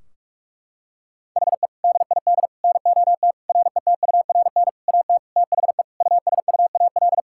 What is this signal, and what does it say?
Morse code, heard as a keyed tone, about 42 wpm, 700 Hz. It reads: HE DID NOT RETURN AT THE USUAL
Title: Sentences from Top 1600 Words 45wpm